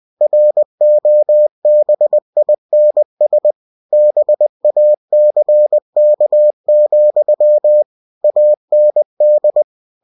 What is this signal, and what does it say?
Morse code, heard as a keyed tone, about 20 wpm, 600 Hz. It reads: ROBINS BACK, AND
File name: cw-books / winnie-the-pooh / winnie-20wpm-page-0076